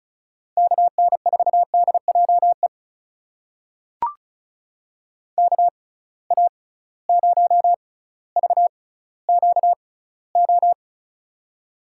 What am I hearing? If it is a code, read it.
Morse code, heard as a keyed tone, about 36 wpm, spaced as for 18 wpm, 700 Hz. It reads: KN4DJE KA0VQO
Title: USA Call Signs - 6 Characters - 6x Character Spacing - 35wpm